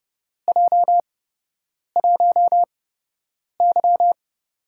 Morse code, heard as a keyed tone, about 30 wpm, spaced as for 8 wpm, 700 Hz. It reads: J1Y
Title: USA Call Signs - 4 Characters - 8x Character Spacing - 30wpm